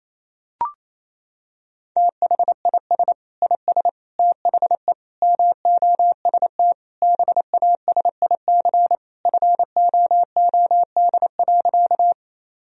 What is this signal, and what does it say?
Morse code, heard as a keyed tone, about 28 wpm, 700 Hz. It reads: THIS IS THE MOST BASIC FOOD.
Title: Sentences from Top 1000 Words - Review Part 3 - 28wpm